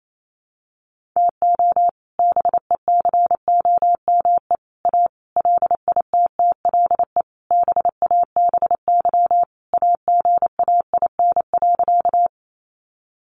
Morse code, heard as a keyed tone, about 28 wpm, 700 Hz. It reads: TO BECOME A LITTLE BABY AGAIN.
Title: Sentences from Top 1300 Words 28wpm